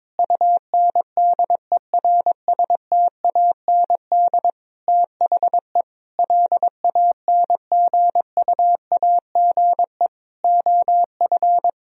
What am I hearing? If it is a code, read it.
Morse code, heard as a keyed tone, about 22 wpm, 700 Hz. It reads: UNDERSTAND THE LANGUAGE OF
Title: Sentences from Top 1000 Words - Review Part 3 - 22wpm